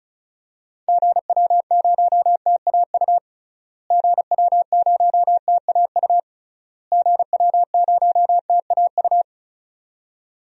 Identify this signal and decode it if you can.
Morse code, heard as a keyed tone, about 36 wpm, 700 Hz. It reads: GW0TAU GW0TAU GW0TAU
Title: DXCC-100 Call Signs - 6 Characters - Repeated 3x - 35wpm